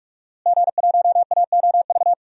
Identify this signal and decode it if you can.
Morse code, heard as a keyed tone, about 47 wpm, 700 Hz. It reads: G1AOV
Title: DXCC-100 Call Signs - 5 Characters - Repeated 3x - 45wpm